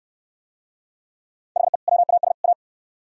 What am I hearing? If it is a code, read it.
Morse code, heard as a keyed tone, about 66 wpm, 700 Hz. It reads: HE WAS A